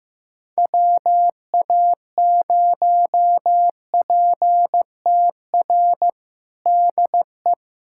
Morse code, heard as a keyed tone, about 15 wpm, 700 Hz. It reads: WA0PTR DE
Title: USA Callsign DE Callsign - 6 Characters - 15wpm